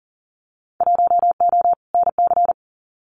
Morse code, heard as a keyed tone, about 40 wpm, 700 Hz. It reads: 1O NC
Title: Winter Field Day Exchanges 40wpm